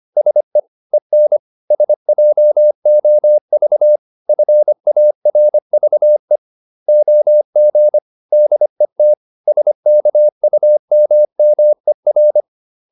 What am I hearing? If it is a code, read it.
Morse code, heard as a keyed tone, about 25 wpm, 600 Hz. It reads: SE EN SJOV FARVE OG DET SKUMMER
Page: Ferie 25 wpm